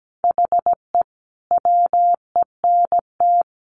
Morse code, heard as a keyed tone, about 17 wpm, 700 Hz. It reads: HE WENT
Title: Sentences from Top 1300 Words 17wpm